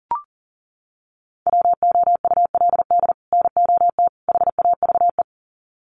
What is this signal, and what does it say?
Morse code, heard as a keyed tone, about 40 wpm, 700 Hz. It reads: WOULD NOT HAVE
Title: Sets of 3 Words 40wpm